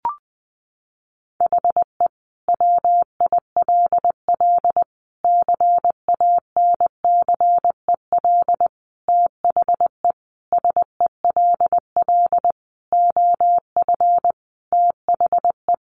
Morse code, heard as a keyed tone, about 20 wpm, 700 Hz. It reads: HE WILL CANCEL THE SELL OF THE
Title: Sentences from Top 1200 Words 20wpm